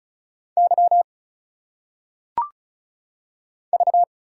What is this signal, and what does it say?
Morse code, heard as a keyed tone, about 36 wpm, 700 Hz. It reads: Y V
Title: Single Letters - Phonetic Pronunciation 35wpm